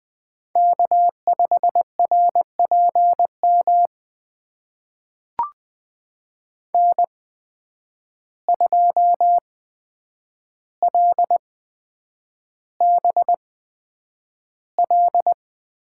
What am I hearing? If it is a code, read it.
Morse code, heard as a keyed tone, about 20 wpm, spaced as for 9 wpm, 700 Hz. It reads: K5RPM N2LBL
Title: USA Call Signs - 5 Characters - 8x Character Spacing - 20wpm